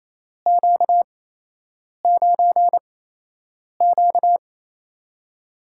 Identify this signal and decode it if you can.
Morse code, heard as a keyed tone, about 28 wpm, spaced as for 8 wpm, 700 Hz. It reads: Q9Q
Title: USA Call Signs - 4 Characters - 8x Character Spacing - 28wpm